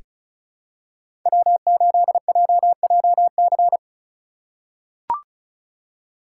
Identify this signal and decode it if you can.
Morse code, heard as a keyed tone, about 36 wpm, 700 Hz. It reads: W8JJC
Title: USA Call Signs - 3x Character Spacing - 35wpm